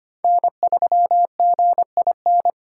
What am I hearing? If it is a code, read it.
Morse code, heard as a keyed tone, about 25 wpm, 700 Hz. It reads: N3GIN
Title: USA Call Signs - 5 Characters - 3x Character Spacing - 25wpm